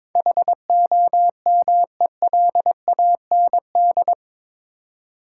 Morse code, heard as a keyed tone, about 22 wpm, 700 Hz. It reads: HOMELAND
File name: practice / 22wpm / Eight Letter Words - 22wpm